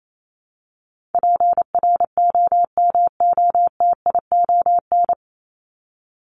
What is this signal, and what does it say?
Morse code, heard as a keyed tone, about 28 wpm, 700 Hz. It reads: PROMOTION
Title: Top 1501-1750 Words 28wpm